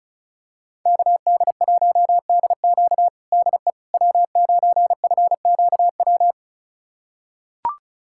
Morse code, heard as a keyed tone, about 36 wpm, 700 Hz. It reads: KD1DQ DE W9FQW
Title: USA Callsign DE Callsign - 5 Characters - 35wpm